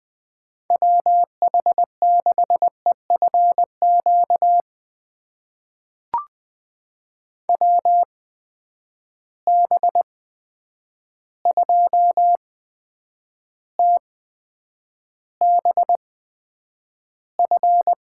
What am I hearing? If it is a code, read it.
Morse code, heard as a keyed tone, about 20 wpm, spaced as for 9 wpm, 700 Hz. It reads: WH6EFQ WB2TBF
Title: USA Call Signs - 8x Character Spacing - 20wpm